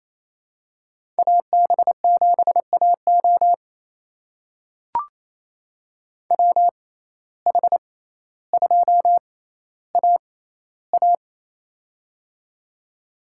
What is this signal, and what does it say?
Morse code, heard as a keyed tone, about 28 wpm, spaced as for 14 wpm, 700 Hz. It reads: AB7AO WH2AA
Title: USA Call Signs - 5 Characters - 6x Character Spacing - 28wpm